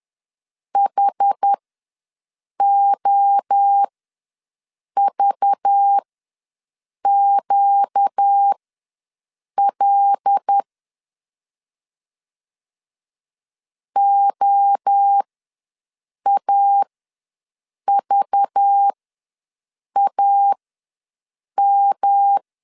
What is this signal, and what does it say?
Morse code, heard as a keyed tone, about 10 wpm, spaced as for 6 wpm, 800 Hz. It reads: HOVQL OAVAM
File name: LEZIONI / 5 LBVC / 5 LBV ws 10wpm 3645